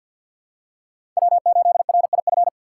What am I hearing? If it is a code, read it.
Morse code, heard as a keyed tone, about 50 wpm, 700 Hz. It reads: W8RIF
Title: USA Call Signs - 5 Characters - 6x Character Spacing - 50wpm